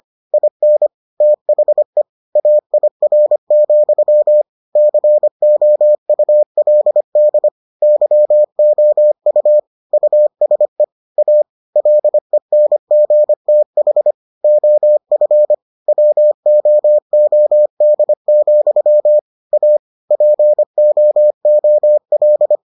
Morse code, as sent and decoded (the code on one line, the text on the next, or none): .. -. / - .... / .- .. .-. --..-- / -.-. --- ..- .-.. -.. / -.-- --- ..- / ..- ... / .- / .-.. -. --. - .... / --- ..-. / .-- --- --- -.. --..-- / .- / .--. --- --- .-..
IN THE AIR, COULD YOU USE A LENGTH OF WOOD, A POOL